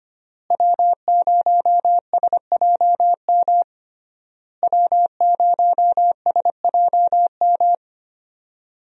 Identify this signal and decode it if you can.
Morse code, heard as a keyed tone, about 25 wpm, 700 Hz. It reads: W0SJM W0SJM
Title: USA Call Signs - 5 Characters - Repeated 3x - 25wpm